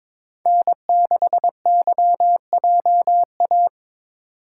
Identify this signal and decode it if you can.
Morse code, heard as a keyed tone, about 27 wpm, 700 Hz. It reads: N6YJA
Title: Speed Racing - USA 1x3 Call Signs 22wpm